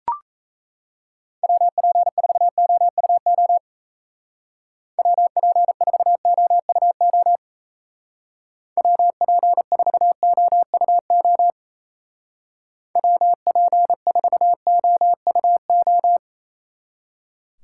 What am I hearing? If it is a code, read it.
Morse code, heard as a keyed tone, about 35 wpm, 700 Hz. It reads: WP4OUO WP4OUO WP4OUO WP4OUO
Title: Speed Racing - USA 2x3 Call Signs 28wpm